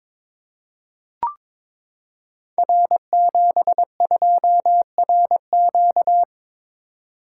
Morse code, heard as a keyed tone, about 22 wpm, 700 Hz. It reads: R72RQ
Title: Code Groups - 5 Letter:Numbers - 22wpm